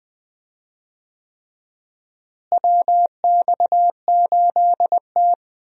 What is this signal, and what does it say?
Morse code, heard as a keyed tone, about 20 wpm, 690 Hz. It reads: WX8T
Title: USA Call Signs - 4 Characters - Repeated 3x - 20wpm